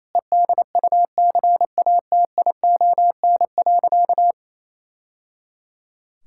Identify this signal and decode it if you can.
Morse code, heard as a keyed tone, about 28 wpm, 700 Hz. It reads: EDUCATION.
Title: Sentences from Top 1000 Words - Review Part 6 - 28wpm